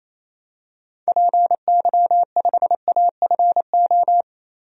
Morse code, heard as a keyed tone, about 28 wpm, 700 Hz. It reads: PY5AFO
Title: DXCC-100 Call Signs - 6 Characters - 28wpm